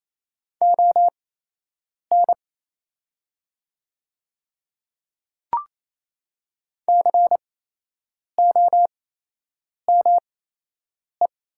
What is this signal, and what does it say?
Morse code, heard as a keyed tone, about 28 wpm, spaced as for 8 wpm, 700 Hz. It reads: ON COME
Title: 8x Character Spacing - Top 100 Words - 28wpm